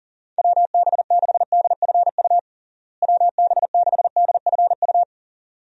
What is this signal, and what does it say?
Morse code, heard as a keyed tone, about 40 wpm, 700 Hz. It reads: WB6DFU WB6DFU
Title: USA Call Signs - 6 Characters - Repeated 2x - 40wpm